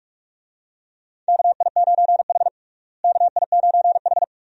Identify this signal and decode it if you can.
Morse code, heard as a keyed tone, about 46 wpm, 700 Hz. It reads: KI9H KI9H
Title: USA Call Signs - 4 Characters - Repeated 2x - 45wpm